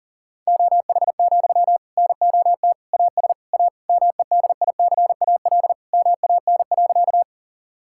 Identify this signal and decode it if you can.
Morse code, heard as a keyed tone, about 40 wpm, 700 Hz. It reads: OH, NOT AS A MEDICAL MAN.
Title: Sentences from Top 1000 Words - Review Part 5 - 40wpm